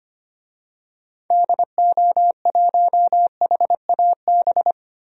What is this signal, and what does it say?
Morse code, heard as a keyed tone, about 25 wpm, 700 Hz. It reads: DO1HAB
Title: DXCC-100 Call Signs - 6 Characters - 25wpm